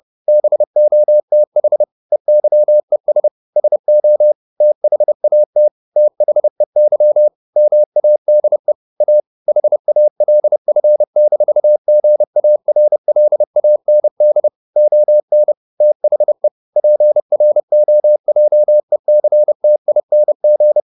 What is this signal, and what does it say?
Morse code, heard as a keyed tone, about 30 wpm, 600 Hz. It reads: BOTH EYES SO THAT THEY MADE A HALF-GARLAND ON THE PROJECTING
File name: the old man and the sea / the old man and the sea 1 30wpm